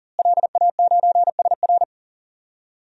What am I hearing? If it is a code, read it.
Morse code, heard as a keyed tone, about 40 wpm, 700 Hz. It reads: LA9SR
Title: DXCC-100 Call Signs - 5 Characters - Repeated 3x - 40wpm